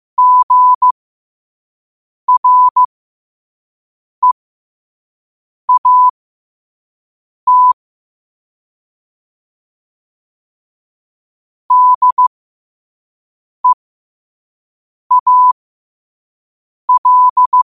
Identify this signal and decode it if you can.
Morse code, heard as a keyed tone, about 15 wpm, spaced as for 5 wpm, 1000 Hz. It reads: GREAT DEAL
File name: qotdfiles / 5wpm